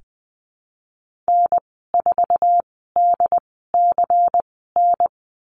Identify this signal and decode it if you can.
Morse code, heard as a keyed tone, about 19 wpm, spaced as for 14 wpm, 700 Hz. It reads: N4DCN
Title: 122 Lesson 20g - Number 3 - Call Signs 20wpm